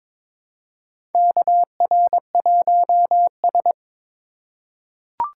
Morse code, heard as a keyed tone, about 22 wpm, 700 Hz. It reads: KR1S
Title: USA Call Signs - 4 Characters - 4x Character Spacing - 22wpm